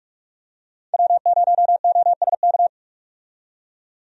Morse code, heard as a keyed tone, about 44 wpm, 700 Hz. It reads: W0OSK
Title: USA Call Signs - 5 Characters - 45wpm